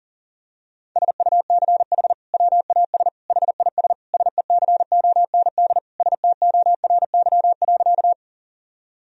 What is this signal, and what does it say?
Morse code, heard as a keyed tone, about 40 wpm, 700 Hz. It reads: SUCH WAS HIS SECOND STORY.
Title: Sentences from Top 1000 Words - Review Part 1 - 40wpm